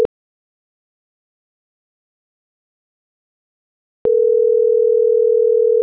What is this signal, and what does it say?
Ringback tone (North America), 440 + 480 Hz.
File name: CENTRAL NAVIERAS / out alejandro.cruz